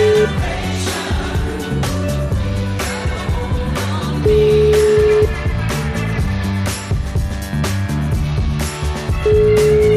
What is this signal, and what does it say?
Ringback tone (Europe, 425 Hz), with noise.